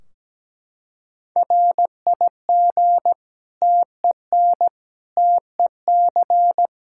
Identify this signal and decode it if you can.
Morse code, heard as a keyed tone, about 17 wpm, 700 Hz. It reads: RIG TEN TEC
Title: Practice QSO - Exchange 2 - Info - 17wpm